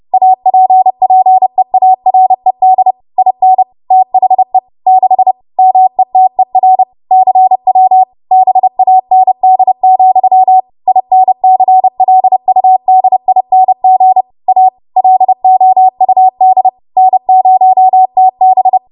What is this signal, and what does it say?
Morse code, heard as a keyed tone, about 30 wpm, 750 Hz. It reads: APPEARED IN THE 6 METER CW BAND, INCLUDING A LOUD N0TB
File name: Morse / 30WPM